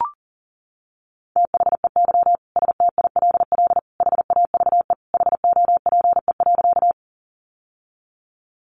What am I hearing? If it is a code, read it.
Morse code, heard as a keyed tone, about 40 wpm, 700 Hz. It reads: THEY STILL HAVE HOPE.
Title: Sentences from Top 1000 Words 40wpm